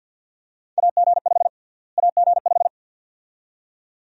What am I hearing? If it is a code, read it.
Morse code, heard as a keyed tone, about 51 wpm, 700 Hz. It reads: AG5 AG5
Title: USA Call Sign Prefixes - Repeated 2x - 50wpm